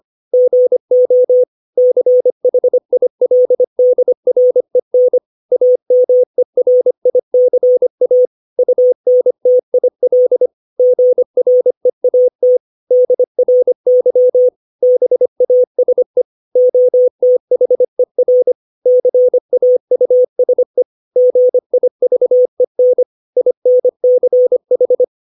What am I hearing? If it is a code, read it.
Morse code, heard as a keyed tone, about 25 wpm, 500 Hz. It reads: GO CHILDREN AMERICA UNTIL GREAT DRY BASE OTHER CAUSE GIVEN INCH
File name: 632 words / 632 Words 25WPM